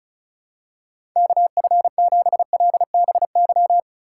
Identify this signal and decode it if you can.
Morse code, heard as a keyed tone, about 36 wpm, 700 Hz. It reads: KF7LBY
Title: USA Call Signs - 6 Characters - 3x Character Spacing - 35wpm